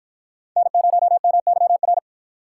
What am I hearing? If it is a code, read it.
Morse code, heard as a keyed tone, about 53 wpm, 700 Hz. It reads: N0MYR